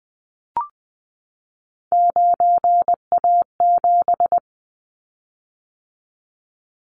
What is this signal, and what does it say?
Morse code, heard as a keyed tone, about 20 wpm, 700 Hz. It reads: 9A7
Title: Callsign Prefixes for Top 25 DXCC - 20wpm